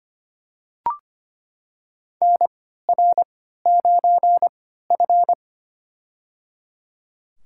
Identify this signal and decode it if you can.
Morse code, heard as a keyed tone, about 25 wpm, spaced as for 14 wpm, 700 Hz. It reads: NR9F
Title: USA Call Signs - 4 Characters - 3x Character Spacing - 25wpm